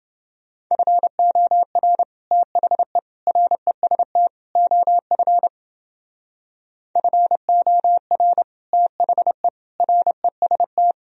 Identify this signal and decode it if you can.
Morse code, heard as a keyed tone, about 30 wpm, 700 Hz. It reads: FOR THE REST OF FOR THE REST